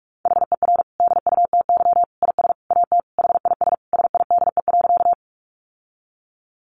Morse code, heard as a keyed tone, about 44 wpm, 700 Hz. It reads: HER DUTY IS AT HIS SIDE.
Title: Sentences from Top 1500 Words 45wpm